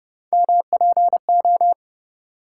Morse code, heard as a keyed tone, about 34 wpm, 700 Hz. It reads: MPO